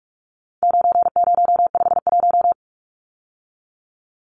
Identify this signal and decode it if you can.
Morse code, heard as a keyed tone, about 44 wpm, 700 Hz. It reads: 9051
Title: Numbers - 4-Digit 45wpm